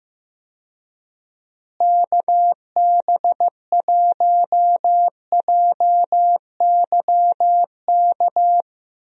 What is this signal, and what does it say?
Morse code, heard as a keyed tone, about 15 wpm, 700 Hz. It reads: KB1JYK
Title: USA Call Signs - 6 Characters - Repeated 2x - 15wpm